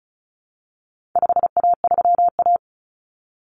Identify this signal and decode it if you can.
Morse code, heard as a keyed tone, about 34 wpm, 700 Hz. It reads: 5A3A
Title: Common Cut Numbers - 4-Digit 35wpm